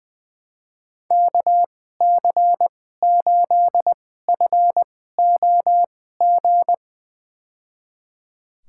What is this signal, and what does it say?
Morse code, heard as a keyed tone, about 20 wpm, spaced as for 14 wpm, 700 Hz. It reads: KC8FOG
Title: USA Call Signs - 6 Characters - 2x Character Spacing - 20wpm